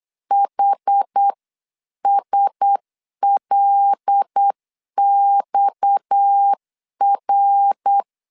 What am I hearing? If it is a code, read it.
Morse code, heard as a keyed tone, about 8 wpm, 800 Hz. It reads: HSLXR
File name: CW / DET / 1 6 1500C S C 8wpm 5219